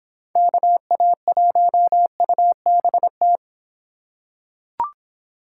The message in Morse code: KA1UBT

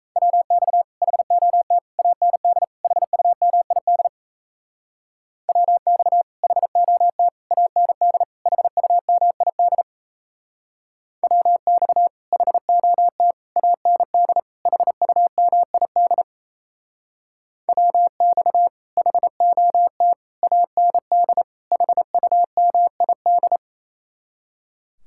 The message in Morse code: WX HOT AND HUMID WX HOT AND HUMID WX HOT AND HUMID WX HOT AND HUMID

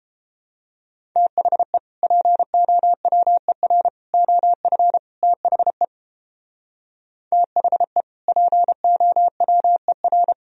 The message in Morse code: THE POWER OF THE THE POWER

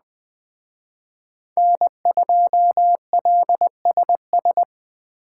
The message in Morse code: N2LSS

